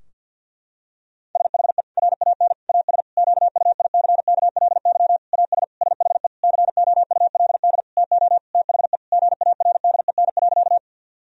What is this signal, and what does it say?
Morse code, heard as a keyed tone, about 50 wpm, 700 Hz. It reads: SHE RAN AS QUICKLY AS SHE COULD TO THE GARDEN.
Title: Sentences from Top 1000 Words - Review Part 4 - 50wpm